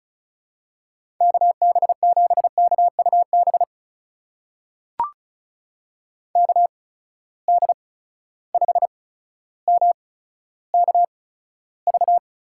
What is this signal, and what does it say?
Morse code, heard as a keyed tone, about 36 wpm, spaced as for 15 wpm, 700 Hz. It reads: KB7KUB KD5MKV
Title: USA Call Signs - 6 Characters - 8x Character Spacing - 35wpm